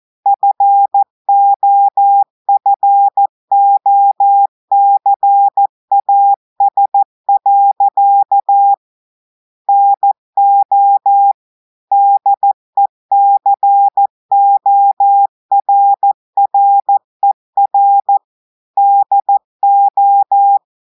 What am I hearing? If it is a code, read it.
Morse code, heard as a keyed tone, about 14 wpm, 800 Hz. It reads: FOFOCAS. NO DECORRER DO
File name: audios / portugais / portugais texte en clair 2 vitesse 14 mots minutes